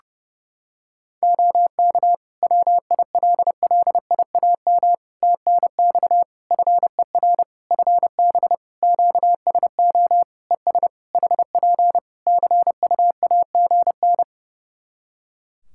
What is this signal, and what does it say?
Morse code, heard as a keyed tone, about 30 wpm, 700 Hz. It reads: OK WILLIAM TNX FER FB QSO ES HP CUAGN
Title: Practice QSO - Exchange 4 - Ending - 30wpm